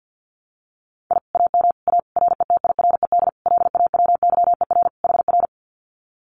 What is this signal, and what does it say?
Morse code, heard as a keyed tone, about 50 wpm, 700 Hz. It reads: I AM A RETIRED LAWYER HR
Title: Common QSO Phrases Encore 50wpm